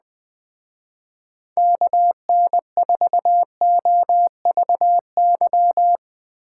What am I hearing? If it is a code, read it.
Morse code, heard as a keyed tone, about 20 wpm, 700 Hz. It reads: KN4OVY
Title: Speed Racing - USA 2x3 Call Signs 17wpm